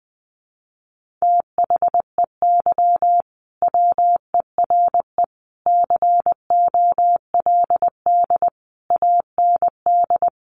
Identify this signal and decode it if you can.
Morse code, heard as a keyed tone, about 20 wpm, 700 Hz. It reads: THEY WERE COLD AND
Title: Sentences from Top 1500 Words 20wpm